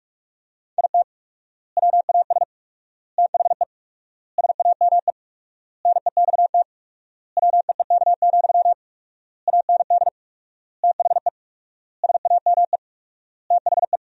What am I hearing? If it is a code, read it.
Morse code, heard as a keyed tone, about 46 wpm, 700 Hz. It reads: IT WAS THE SAME NEXT WEEK, AND THE SAME THE
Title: 4x Word-Spacing - Sentences from Top 500 Words - 45wpm